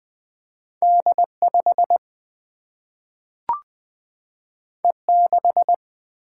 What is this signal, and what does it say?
Morse code, heard as a keyed tone, about 20 wpm, 700 Hz. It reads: D5 E6
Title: Single-Letter Single-Number 20wpm